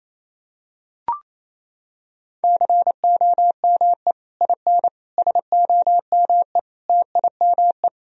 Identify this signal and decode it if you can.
Morse code, heard as a keyed tone, about 27 wpm, 700 Hz. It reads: COME IN SOME TIME
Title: Sentences from Top 1000 Words - Review Part 2 - 28wpm